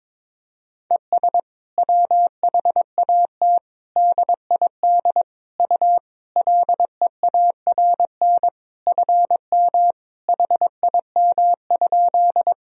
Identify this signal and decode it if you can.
Morse code, heard as a keyed tone, about 22 wpm, 700 Hz. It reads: ES WHAT DID U LEARN FM HIM?
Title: Sentences from Top 300 Words and Abbreviations 22wpm